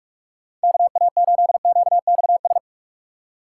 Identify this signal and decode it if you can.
Morse code, heard as a keyed tone, about 49 wpm, 700 Hz. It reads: KA8QXS